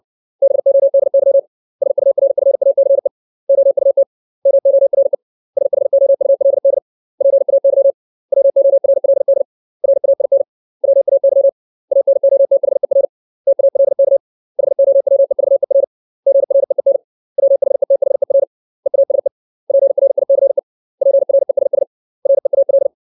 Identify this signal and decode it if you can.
Morse code, heard as a keyed tone, about 60 wpm, 550 Hz. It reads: BODY SURFACE OUT MORE SHOULD PAY WORLD RAIN WAY ANOTHER TALK HOWEVER GREEN WHETHER EASE PRICE PRESS REAL
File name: 632 words / 632 Words 60WPM